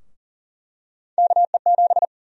Morse code, heard as a keyed tone, about 40 wpm, 700 Hz. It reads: KE7
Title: USA Call Sign Prefixes - Repeated 2x - 40wpm